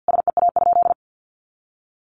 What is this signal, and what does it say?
Morse code, heard as a keyed tone, about 50 wpm, 700 Hz. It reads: SEA?
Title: Sentences from Top 1400 Words 50wpm